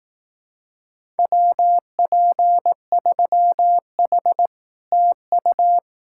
Morse code, heard as a keyed tone, about 18 wpm, 700 Hz. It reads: WP3H TU
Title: Speed-Racing Common QSO Phrases Encore - 15wpm